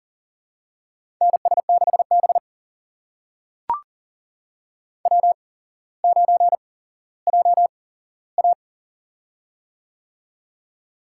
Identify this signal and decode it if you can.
Morse code, heard as a keyed tone, about 38 wpm, spaced as for 17 wpm, 700 Hz. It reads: NS6B W9JA
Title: USA Call Signs - 4 Characters - 8x Character Spacing - 40wpm